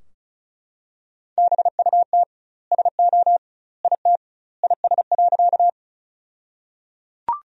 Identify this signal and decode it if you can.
Morse code, heard as a keyed tone, about 36 wpm, 700 Hz. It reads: BUT SO IT IS.
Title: 2x Word-Spacing - Sentences from Top 100 Words - 35wpm